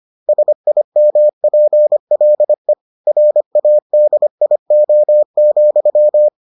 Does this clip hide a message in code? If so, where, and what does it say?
Yes: Morse code from 0.3 to 6.3 s, heard as a keyed tone, about 25 wpm, 600 Hz. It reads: SIMPLE RADIO,